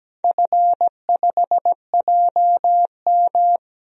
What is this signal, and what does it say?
Morse code, heard as a keyed tone, about 17 wpm, 700 Hz. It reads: F5JM
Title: DXCC-100 Call Signs - 4 Characters - 17wpm